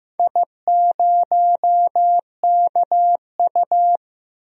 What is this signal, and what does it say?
Morse code, heard as a keyed tone, about 15 wpm, 700 Hz. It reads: I0KU
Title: DXCC-100 Call Signs - 4 Characters - 15wpm